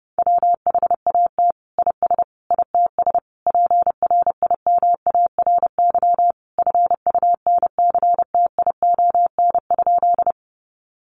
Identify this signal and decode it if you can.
Morse code, heard as a keyed tone, about 30 wpm, 700 Hz. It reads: WHAT IS ITS PRIMARY FUNCTION?
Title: Sentences from Top 1100 Words 30wpm